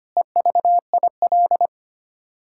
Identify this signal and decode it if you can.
Morse code, heard as a keyed tone, about 25 wpm, 700 Hz. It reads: EVIL
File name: Sets of 9 Words - 25wpm